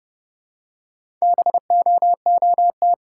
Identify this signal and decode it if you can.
Morse code, heard as a keyed tone, about 32 wpm, 700 Hz. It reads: BOOT
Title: Four Letter Words - 30wpm